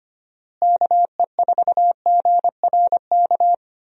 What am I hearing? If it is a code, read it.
Morse code, heard as a keyed tone, about 25 wpm, 700 Hz. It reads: KE4GRK
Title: USA Call Signs - 6 Characters - Repeated 2x - 25wpm